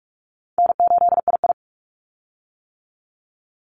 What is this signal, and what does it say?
Morse code, heard as a keyed tone, about 44 wpm, 700 Hz. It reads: N8II
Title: Call Signs - 45wpm